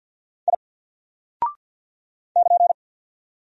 Morse code, heard as a keyed tone, about 48 wpm, 700 Hz. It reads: I <KN>
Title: Single Character (Letter:Number:Punctuation:Prosign) 50wpm